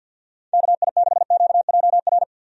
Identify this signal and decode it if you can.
Morse code, heard as a keyed tone, about 50 wpm, 700 Hz. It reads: KI6QJR
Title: USA Call Signs - 6 Characters - 50wpm